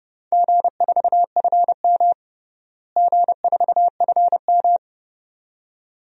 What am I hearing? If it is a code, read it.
Morse code, heard as a keyed tone, about 30 wpm, 700 Hz. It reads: G4FM G4FM
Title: DXCC-100 Call Signs - 4 Characters - Repeated 2x - 30wpm